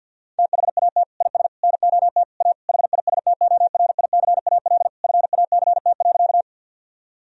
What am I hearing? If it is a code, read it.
Morse code, heard as a keyed tone, about 50 wpm, 700 Hz. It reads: THAT IS NOT A HISTORICAL FACT.
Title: Sentences from Top 1000 Words - Review Part 4 - 50wpm